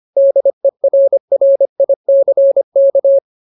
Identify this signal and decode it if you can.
Morse code, heard as a keyed tone, about 25 wpm, 550 Hz. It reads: DERRICK
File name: Ultimate Word List / 25WPM Ultimate